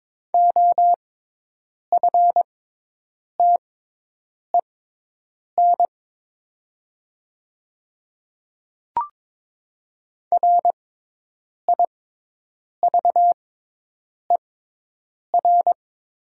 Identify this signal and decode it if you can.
Morse code, heard as a keyed tone, about 22 wpm, spaced as for 8 wpm, 700 Hz. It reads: OFTEN RIVER
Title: 6x Character Spacing - Top 300 Words - 22wpm